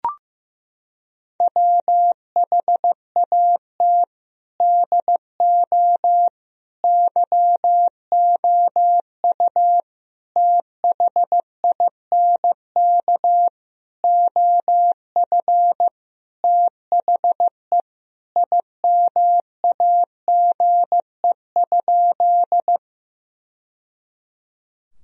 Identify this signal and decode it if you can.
Morse code, heard as a keyed tone, about 16 wpm, 700 Hz. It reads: WHAT DO YOU THINK OF THE IMAGE?
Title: Sentences from Top 1000 Words - Review Part 4 - 15wpm